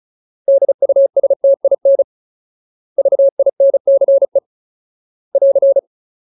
Morse code, heard as a keyed tone, about 34 wpm, 550 Hz. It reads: DUSTIN VINCE <AR>
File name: Ultimate Word List / 35WPM Ultimate